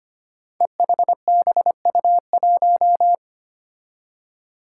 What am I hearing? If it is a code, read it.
Morse code, heard as a keyed tone, about 25 wpm, 700 Hz. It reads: EHBU1
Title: Five Challenging Letter-Number Combinations 25wpm